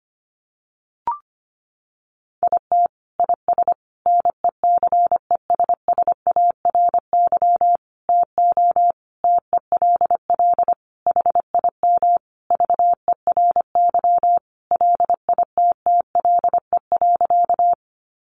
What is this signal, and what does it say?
Morse code, heard as a keyed tone, about 25 wpm, 700 Hz. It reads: IT IS NECESSARY TO TELL HIM VERY LITTLE.
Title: Sentences from Top 800 Words 25wpm